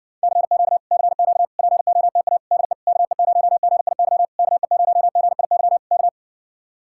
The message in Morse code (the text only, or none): CQ CQ POTA DE KE0GIY KE0GIY K